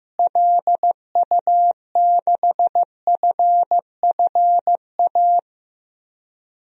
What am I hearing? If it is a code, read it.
Morse code, heard as a keyed tone, about 16 wpm, 700 Hz. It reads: LU6FFA
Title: DXCC-100 Call Signs - 6 Characters - Repeated 3x - 15wpm